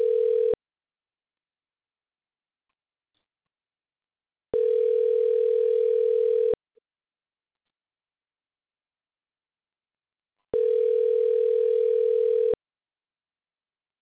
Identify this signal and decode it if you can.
Ringback tone (North America), 440 + 480 Hz.